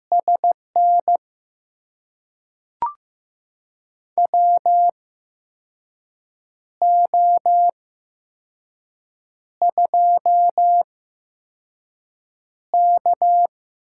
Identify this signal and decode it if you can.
Morse code, heard as a keyed tone, about 15 wpm, 700 Hz. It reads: S N W O 2 K
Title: USA Call Signs - 4 Characters - 8x Character Spacing - 15wpm